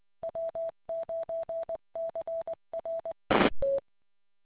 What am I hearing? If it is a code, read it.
Morse code, heard as a keyed tone, about 20 wpm, 660 Hz. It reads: W9CRE